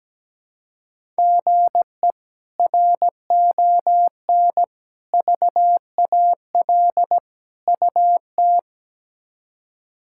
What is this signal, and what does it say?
Morse code, heard as a keyed tone, about 17 wpm, 700 Hz. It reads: GE RON VAL UT
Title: K1USN SST QSO Practice - 17wpm